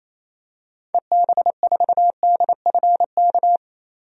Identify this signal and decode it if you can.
Morse code, heard as a keyed tone, about 27 wpm, 700 Hz. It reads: EB4DFK
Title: DXCC-100 Call Signs - 6 Characters - 28wpm